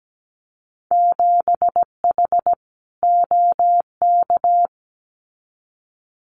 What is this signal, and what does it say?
Morse code, heard as a keyed tone, about 17 wpm, 700 Hz. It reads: 7H OK
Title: Winter Field Day Exchanges Encore 17wpm